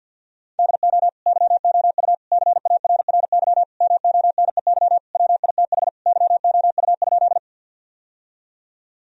Morse code, heard as a keyed tone, about 48 wpm, 700 Hz. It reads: DO YOU CARRY MONEY WITH YOU?
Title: Sentences from Top 1000 Words - Review Part 1 - 50wpm